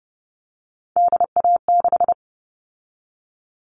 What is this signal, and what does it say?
Morse code, heard as a keyed tone, about 30 wpm, 700 Hz. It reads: DA6
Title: All Cut Numbers - 3-Digit 30wpm